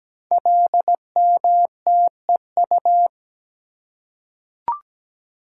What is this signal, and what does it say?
Morse code, heard as a keyed tone, about 17 wpm, 700 Hz. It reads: LMTEU
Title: Code Groups - 5 Letters - 17wpm